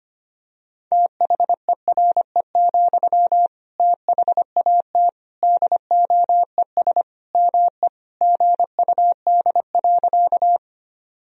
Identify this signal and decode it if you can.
Morse code, heard as a keyed tone, about 25 wpm, 700 Hz. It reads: THERE, THAT DOES ME GUD.
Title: Sentences from Top 200 Words and Abbreviations 25wpm